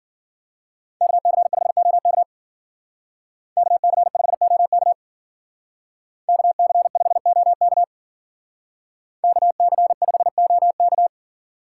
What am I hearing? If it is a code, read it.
Morse code, heard as a keyed tone, about 48 wpm, 700 Hz. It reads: KC5OK KC5OK KC5OK KC5OK
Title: Speed Racing - USA 2x2 Call Signs 40wpm